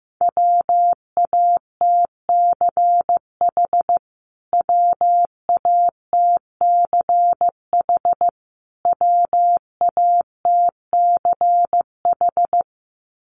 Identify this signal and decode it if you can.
Morse code, heard as a keyed tone, about 15 wpm, 700 Hz. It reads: WATCH WATCH WATCH
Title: Top 500 Words Repeated 15wpm